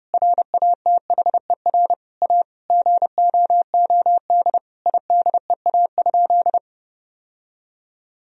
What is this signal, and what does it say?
Morse code, heard as a keyed tone, about 32 wpm, 700 Hz. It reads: RATHER A GOOD IDEA?
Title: Sentences from Top 1000 Words - Review Part 4 - 30wpm